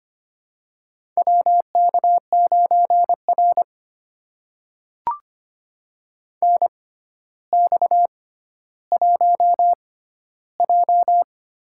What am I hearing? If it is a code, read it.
Morse code, heard as a keyed tone, about 25 wpm, spaced as for 13 wpm, 700 Hz. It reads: WK9R NX1J
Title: USA Call Signs - 4 Characters - 6x Character Spacing - 25wpm